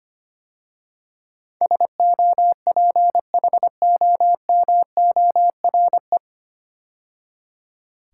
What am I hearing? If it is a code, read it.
Morse code, heard as a keyed tone, about 25 wpm, 690 Hz. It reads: SOPHOMORE
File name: practice / 25wpm / 9 Letter Words - First 3 Letters Given - 25wpm